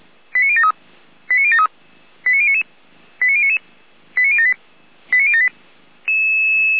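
Five-tone selective call (ZVEI), heard as quick runs of five tones, tones of about 70 ms. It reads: 89073 89073 89099 89099 89078 89078
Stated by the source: Five-tone ZVEI selective call, received with an SDR.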